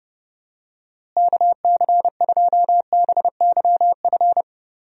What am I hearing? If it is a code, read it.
Morse code, heard as a keyed tone, about 30 wpm, 700 Hz. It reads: KC2BYF
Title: USA Call Signs - 6 Characters - Repeated 3x - 30wpm